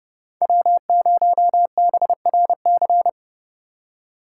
Morse code, heard as a keyed tone, about 37 wpm, 700 Hz. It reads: W0BRC
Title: Speed Racing - USA 1x3 Call Signs 30wpm